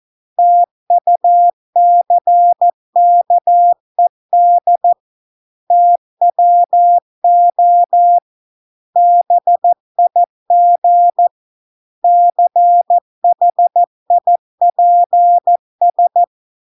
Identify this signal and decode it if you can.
Morse code, heard as a keyed tone, about 14 wpm, 700 Hz. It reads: TUCKED TWO BIG CHIPS